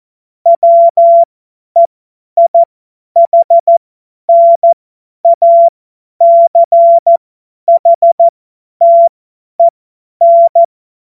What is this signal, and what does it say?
Morse code, heard as a keyed tone, about 14 wpm, spaced as for 10 wpm, 680 Hz. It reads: WEIHNACHTEN
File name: Morsekurs / Witze / Witz 03. Januar 2023 50 F70